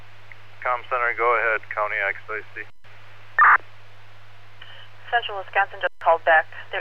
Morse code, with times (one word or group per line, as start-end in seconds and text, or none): none